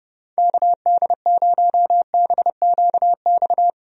Morse code, heard as a keyed tone, about 29 wpm, 700 Hz. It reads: KD0BQX
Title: USA Call Signs - 6 Characters - 30wpm